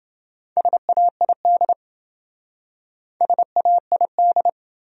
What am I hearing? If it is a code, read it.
Morse code, heard as a keyed tone, about 28 wpm, 700 Hz. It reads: SAID SAID